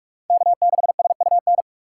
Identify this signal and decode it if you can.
Morse code, heard as a keyed tone, about 60 wpm, spaced as for 53 wpm, 700 Hz. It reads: K6SUN
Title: Speed Racing - USA 1x3 Call Signs 45wpm